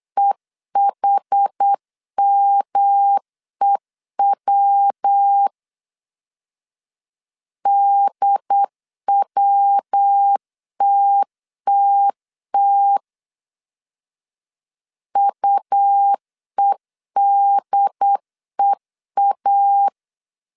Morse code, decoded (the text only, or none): EHMEW DWTTT UEDEA